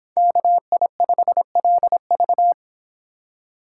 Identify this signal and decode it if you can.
Morse code, heard as a keyed tone, about 30 wpm, 700 Hz. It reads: KI5LV